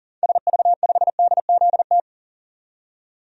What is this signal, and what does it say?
Morse code, heard as a keyed tone, about 40 wpm, 700 Hz. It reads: SV5DZT
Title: DXCC-100 Call Signs - 6 Characters - 40wpm